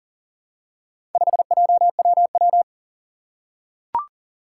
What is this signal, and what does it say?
Morse code, heard as a keyed tone, about 40 wpm, 700 Hz. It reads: 5JWW
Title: Four Challenging Letter-Number Combinations 40wpm